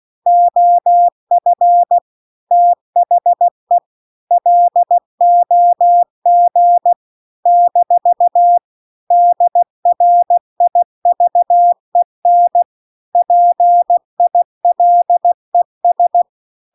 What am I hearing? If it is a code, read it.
Morse code, heard as a keyed tone, about 16 wpm, 700 Hz. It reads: OF THE LOG - DRIVEN PILES
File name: CW-The-Big-Twoheaded-River-1-16WPM